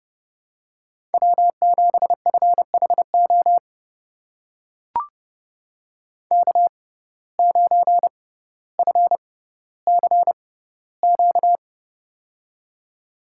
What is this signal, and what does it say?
Morse code, heard as a keyed tone, about 30 wpm, spaced as for 15 wpm, 700 Hz. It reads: W7FHO K9FCQ
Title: USA Call Signs - 5 Characters - 6x Character Spacing - 30wpm